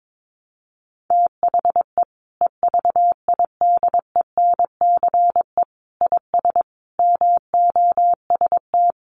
Morse code, heard as a keyed tone, about 22 wpm, 700 Hz. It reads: THE EVIDENCE IS MOST
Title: Sentences from Top 1300 Words 22wpm